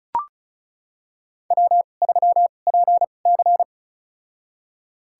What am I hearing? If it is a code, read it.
Morse code, heard as a keyed tone, about 36 wpm, spaced as for 25 wpm, 700 Hz. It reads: W3PC
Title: USA Call Signs - 4 Characters - 2x Character Spacing - 35wpm